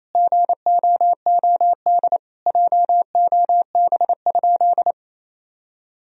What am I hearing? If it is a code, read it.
Morse code, heard as a keyed tone, about 28 wpm, 700 Hz. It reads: GOOD JOB?
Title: Sentences from Top 1000 Words - Review Part 3 - 28wpm